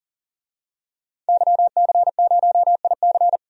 Morse code, heard as a keyed tone, about 40 wpm, 700 Hz. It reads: YC0IC
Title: DXCC-100 Call Signs - 5 Characters - 40wpm